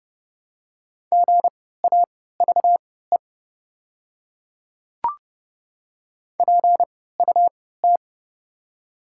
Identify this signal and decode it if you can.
Morse code, heard as a keyed tone, about 30 wpm, spaced as for 17 wpm, 700 Hz. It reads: GAVE PUT
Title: 3x Character Spacing - Top 500 Words - 30wpm